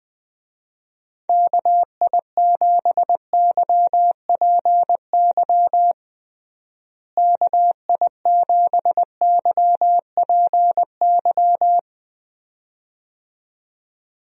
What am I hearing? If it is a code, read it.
Morse code, heard as a keyed tone, about 20 wpm, 700 Hz. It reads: KI7YPY KI7YPY
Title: USA Call Signs - 6 Characters - Repeated 2x - 20wpm